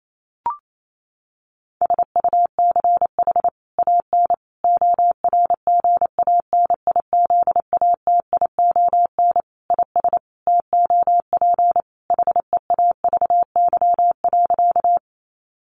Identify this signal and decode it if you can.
Morse code, heard as a keyed tone, about 28 wpm, 700 Hz. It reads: SUCH AN ORGANIZATION IS TOP HEAVY.
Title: Sentences from Top 1100 Words 28wpm